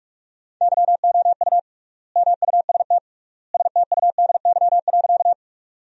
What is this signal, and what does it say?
Morse code, heard as a keyed tone, about 42 wpm, 700 Hz. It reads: YOU MUST STUDY.
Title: 3x Word-Spacing - Sentences from Top 300 Words - 45wpm